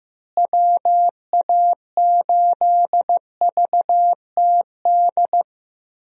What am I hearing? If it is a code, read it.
Morse code, heard as a keyed tone, about 14 wpm, 700 Hz. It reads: WA8VTD
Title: USA Call Signs - 6 Characters - Repeated 2x - 15wpm